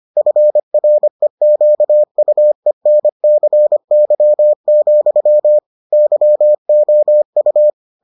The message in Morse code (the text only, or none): FREQUENCY, YOU